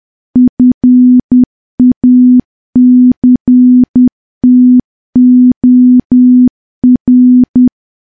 Morse code, heard as a keyed tone, about 10 wpm, 250 Hz. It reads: FACTOR
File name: morse / morse-training-10WPM